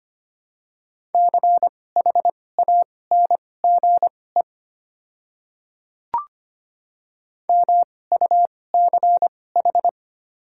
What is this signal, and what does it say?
Morse code, heard as a keyed tone, about 25 wpm, spaced as for 18 wpm, 700 Hz. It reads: CHANGE MUCH
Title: 2x Character Spacing - Top 200 Words - 25wpm